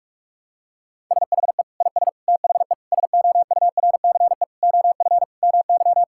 Morse code, heard as a keyed tone, about 46 wpm, 700 Hz. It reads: SHE IS THE SOURCE OF MY
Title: Sentences from Top 1000 Words - Review Part 5 - 45wpm